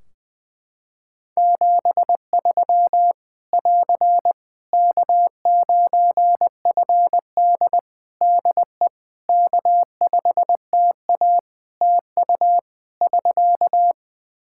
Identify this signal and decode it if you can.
Morse code, heard as a keyed tone, about 20 wpm, 700 Hz. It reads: 73 <AR> K9FD DE K5TA TU <SK>
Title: K1USN SST QSO Practice 20wpm